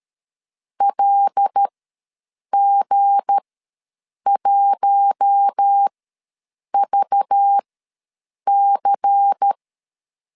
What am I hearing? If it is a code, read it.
Morse code, heard as a keyed tone, about 13 wpm, spaced as for 7 wpm, 800 Hz. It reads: LG1VC
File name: LEZIONI / 7 GF9Z / 7 GF ws 12wpm 3140